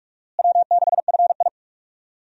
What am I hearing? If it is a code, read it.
Morse code, heard as a keyed tone, about 41 wpm, 700 Hz. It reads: W6FI
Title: USA Call Signs - 4 Characters - Repeated 3x - 45wpm